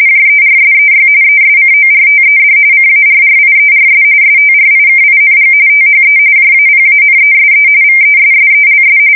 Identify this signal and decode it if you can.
RTTY, about 45.45 baud. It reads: JERSEY ESCORTED CARRIERS STRIKING THE JAPANESE VESSELS
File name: fsk170 / battleshipnj